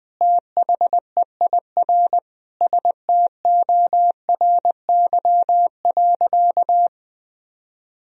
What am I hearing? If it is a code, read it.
Morse code, heard as a keyed tone, about 20 wpm, 700 Hz. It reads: THEIR STORY.
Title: Sentences from Top 1000 Words - Review Part 1 - 20wpm